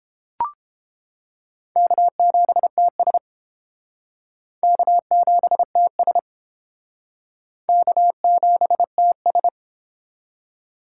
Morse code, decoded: K7TS K7TS K7TS

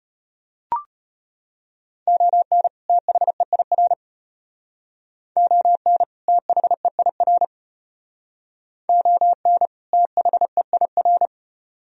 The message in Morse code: ON THEIR ON THEIR ON THEIR

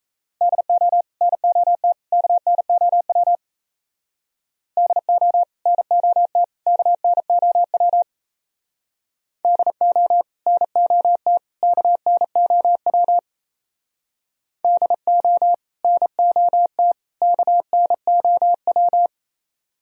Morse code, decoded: DO NOT KNOW DO NOT KNOW DO NOT KNOW DO NOT KNOW

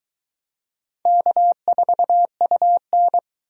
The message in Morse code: K4UN

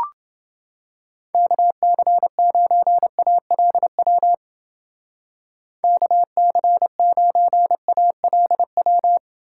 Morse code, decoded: KC9ALW KC9ALW

